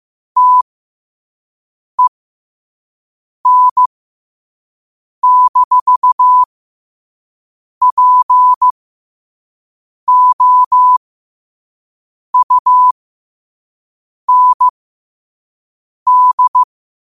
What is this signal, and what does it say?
Morse code, heard as a keyed tone, about 15 wpm, spaced as for 5 wpm, 1000 Hz. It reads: TEN-POUND
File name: qotdfiles / 5wpm